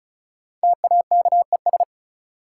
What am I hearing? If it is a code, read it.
Morse code, heard as a keyed tone, about 36 wpm, 700 Hz. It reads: TAKES
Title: Sentences from Top 1000 Words - Review Part 6 - New Word Drill - 35wpm